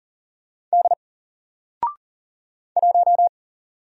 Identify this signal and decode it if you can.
Morse code, heard as a keyed tone, about 40 wpm, 700 Hz. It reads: D 1
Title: Single Character (Letter:Number:Punctuation:Prosign) Alt 40wpm